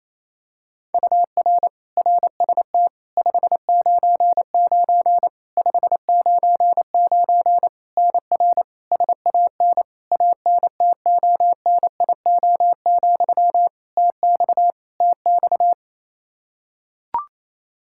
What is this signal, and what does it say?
Morse code, heard as a keyed tone, about 28 wpm, 700 Hz. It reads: UR RST 599 599 NR SAN ANTONIO, TX TX
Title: K1USN SST QSO Practice 28wpm